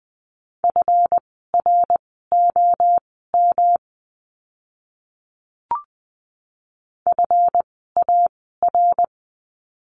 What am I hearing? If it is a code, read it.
Morse code, heard as a keyed tone, about 20 wpm, spaced as for 14 wpm, 700 Hz. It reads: FROM FAR
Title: 139 Lesson 23f - Letter f - Words 20wpm